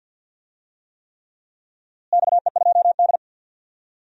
Morse code, heard as a keyed tone, about 51 wpm, 700 Hz. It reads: XE2D
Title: DXCC-100 Call Signs - 4 Characters - Repeated 3x - 50wpm